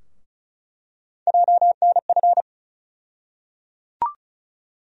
Morse code, heard as a keyed tone, about 36 wpm, 700 Hz. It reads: JNF
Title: USA Call Sign Postfixes - 3 Letters 35wpm